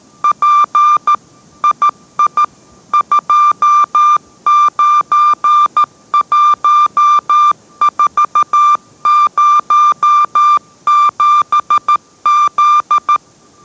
Morse code, heard as a keyed tone, about 13 wpm, 1190 Hz. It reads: P II 291407Z